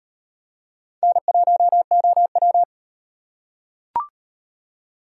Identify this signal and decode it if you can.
Morse code, heard as a keyed tone, about 38 wpm, 700 Hz. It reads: N1OW